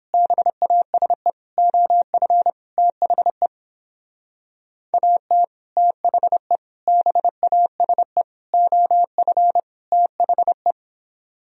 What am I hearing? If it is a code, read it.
Morse code, heard as a keyed tone, about 29 wpm, 700 Hz. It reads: BASE OF THE AT THE BASE OF THE